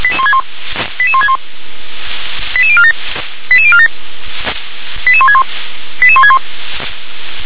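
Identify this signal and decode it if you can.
Five-tone selective call (ZVEI), heard as quick runs of five tones, tones of about 70 ms. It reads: 88161 88047 88047 88161 88161
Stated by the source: ZVEI five-tone call 88036 received with an SDR.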